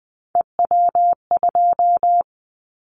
Morse code, heard as a keyed tone, about 19 wpm, 700 Hz. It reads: EW2
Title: Callsign Prefixes for Top 100 DXCC - 20wpm